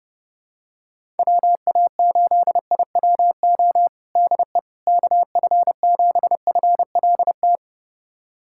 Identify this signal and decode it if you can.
Morse code, heard as a keyed tone, about 30 wpm, 700 Hz. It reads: WA8IWO DE KF7FLT
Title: USA Callsign DE Callsign 30wpm